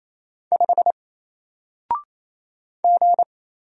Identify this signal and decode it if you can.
Morse code, heard as a keyed tone, about 28 wpm, 700 Hz. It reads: 5 G
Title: Single Letter-Number Alt 28wpm